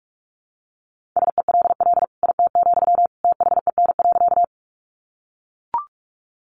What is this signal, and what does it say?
Morse code, heard as a keyed tone, about 44 wpm, 700 Hz. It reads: SELL IT, THEN.
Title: Sentences from Top 1100 Words 45wpm